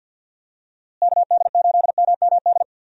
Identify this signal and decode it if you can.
Morse code, heard as a keyed tone, about 52 wpm, 700 Hz. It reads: KD8MMD
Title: USA Call Signs - 6 Characters - Repeated 3x - 50wpm